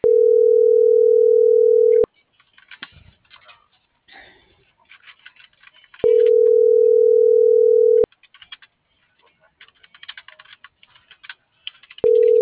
Ringback tone (North America), 440 + 480 Hz.